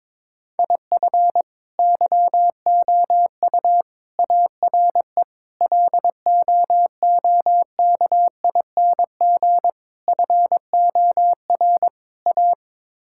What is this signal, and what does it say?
Morse code, heard as a keyed tone, about 22 wpm, 700 Hz. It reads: IF YOU ARE LOOKING FOR A
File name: practice / 22wpm / Sets of 6 Words - 22wpm